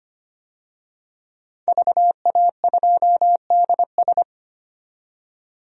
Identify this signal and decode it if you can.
Morse code, heard as a keyed tone, about 25 wpm, 690 Hz. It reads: VA2DS
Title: DXCC-100 Call Signs - 5 Characters - 25wpm